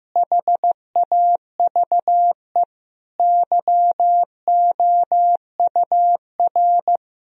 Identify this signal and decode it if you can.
Morse code, heard as a keyed tone, about 15 wpm, 700 Hz. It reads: HAVE YOUR
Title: Sentences from Top 1000 Words - Review Part 4 - 15wpm